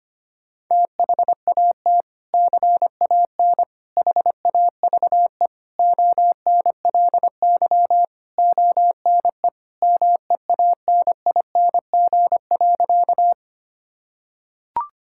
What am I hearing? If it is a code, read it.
Morse code, heard as a keyed tone, about 25 wpm, 700 Hz. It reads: THAT CAN HAVE ONLY ONE MEANING.
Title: Sentences from Top 1000 Words - Review Part 6 - 25wpm